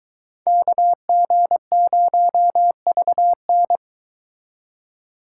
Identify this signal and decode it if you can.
Morse code, heard as a keyed tone, about 22 wpm, 700 Hz. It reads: KG0VN